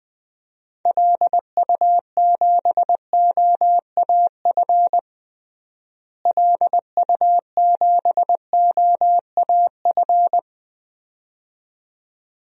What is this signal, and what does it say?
Morse code, heard as a keyed tone, about 21 wpm, 700 Hz. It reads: LU7OAF LU7OAF
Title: DXCC-100 Call Signs - 6 Characters - Repeated 2x - 20wpm